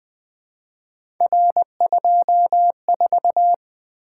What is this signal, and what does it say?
Morse code, heard as a keyed tone, about 20 wpm, 700 Hz. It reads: R24
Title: Three Letter-Number Combinations - 20wpm